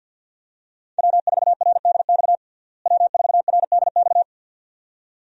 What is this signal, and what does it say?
Morse code, heard as a keyed tone, about 52 wpm, 700 Hz. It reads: W4RDX W4RDX
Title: USA Call Signs - 5 Characters - Repeated 2x - 50wpm